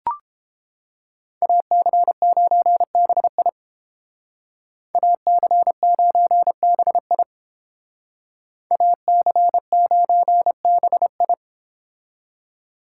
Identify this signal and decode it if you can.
Morse code, heard as a keyed tone, about 30 wpm, 700 Hz. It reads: AC9BI AC9BI AC9BI